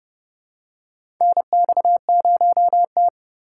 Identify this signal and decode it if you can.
Morse code, heard as a keyed tone, about 30 wpm, 700 Hz. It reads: NX0T
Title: USA Call Signs - 4 Characters - 4x Character Spacing - 30wpm